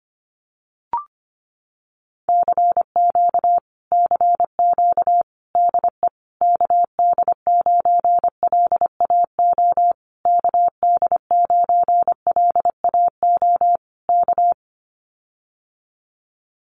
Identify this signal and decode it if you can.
Morse code, heard as a keyed tone, about 25 wpm, 700 Hz. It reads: CQ CQ DE KD9LAO KD9LAO K
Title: Common QSO Phrases Encore 25wpm